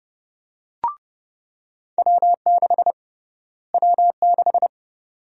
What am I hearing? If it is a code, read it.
Morse code, heard as a keyed tone, about 30 wpm, 700 Hz. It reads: W6 W6
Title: USA Call Sign Prefixes - Repeated 2x - 30wpm